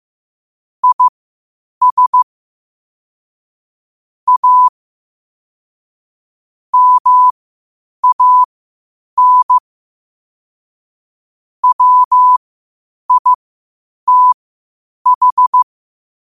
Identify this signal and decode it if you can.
Morse code, heard as a keyed tone, about 15 wpm, spaced as for 8 wpm, 1000 Hz. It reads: IS A MAN WITH
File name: qotdfiles / 8wpm